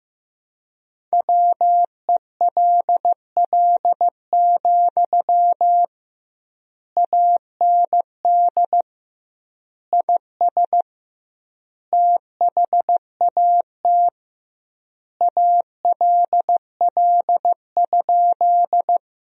Morse code, heard as a keyed tone, about 15 wpm, 700 Hz. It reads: WELL, AND IS THAT ALL?
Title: 2x Word-Spacing - Sentences from Top 200 Words - 15wpm